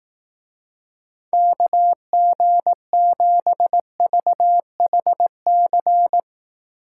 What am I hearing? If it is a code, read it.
Morse code, heard as a keyed tone, about 18 wpm, 700 Hz. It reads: KG7VHC
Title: Speed Racing - USA 2x3 Call Signs 15wpm